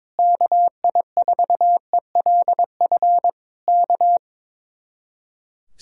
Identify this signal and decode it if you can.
Morse code, heard as a keyed tone, about 22 wpm, 700 Hz. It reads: KI4ELF K
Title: Speed-Racing Common QSO Phrases Encore - 22wpm